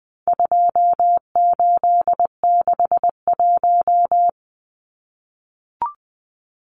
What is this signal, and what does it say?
Morse code, heard as a keyed tone, about 20 wpm, 700 Hz. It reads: 2861
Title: Numbers - 4-Digit 20wpm